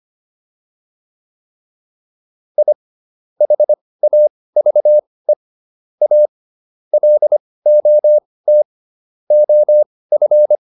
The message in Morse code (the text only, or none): I HAVE A LOT OF